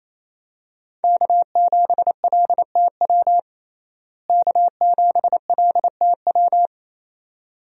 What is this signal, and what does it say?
Morse code, heard as a keyed tone, about 28 wpm, 700 Hz. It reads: K7LTW K7LTW
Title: USA Call Signs - 5 Characters - Repeated 2x - 28wpm